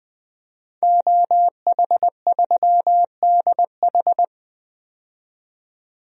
Morse code, heard as a keyed tone, about 20 wpm, 700 Hz. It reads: OH3DH
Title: DXCC-100 Call Signs - 5 Characters - 20wpm